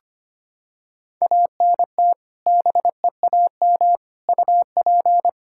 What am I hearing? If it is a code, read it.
Morse code, heard as a keyed tone, about 25 wpm, 700 Hz. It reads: ANT BEAM UP
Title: Practice QSO - Exchange 2 - Info - 25wpm